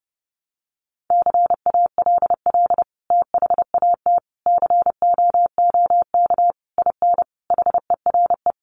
Morse code, heard as a keyed tone, about 30 wpm, 700 Hz. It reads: CALL THAT COOK IN HERE
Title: Sentences from Top 1100 Words 30wpm